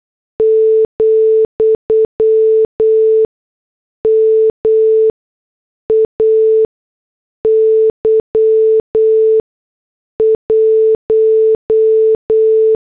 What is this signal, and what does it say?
Morse code, heard as a keyed tone, about 8 wpm, spaced as for 6 wpm, 440 Hz. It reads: , MAY 1
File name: morse / headlines